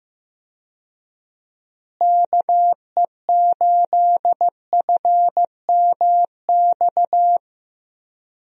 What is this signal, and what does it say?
Morse code, heard as a keyed tone, about 15 wpm, 700 Hz. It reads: KE8FMX
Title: USA Call Signs - 6 Characters - Repeated 3x - 15wpm